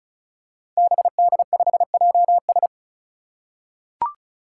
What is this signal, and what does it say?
Morse code, heard as a keyed tone, about 36 wpm, 700 Hz. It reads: BD5JS
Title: Five Challenging Letter-Number Combinations 35wpm